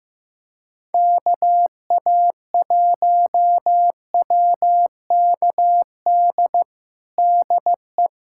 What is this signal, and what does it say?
Morse code, heard as a keyed tone, about 15 wpm, 700 Hz. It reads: KA1WKD DE
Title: USA Callsign DE Callsign - 6 Characters - 15wpm